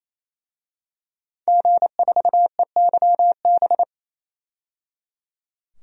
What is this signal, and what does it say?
Morse code, heard as a keyed tone, about 28 wpm, 700 Hz. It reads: G4EYB
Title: DXCC-100 Call Signs - 5 Characters - 28wpm